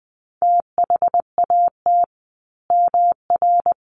Morse code, heard as a keyed tone, about 20 wpm, 700 Hz. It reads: THAT MR
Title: A Sound Of Thunder 20wpm